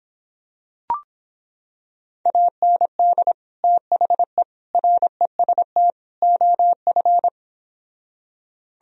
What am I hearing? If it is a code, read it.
Morse code, heard as a keyed tone, about 26 wpm, 700 Hz. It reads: AND THE REST OF